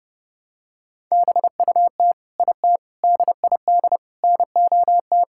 Morse code, heard as a keyed tone, about 32 wpm, 700 Hz. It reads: BUT IT DID NOT
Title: Sentences from Top 1600 Words 30wpm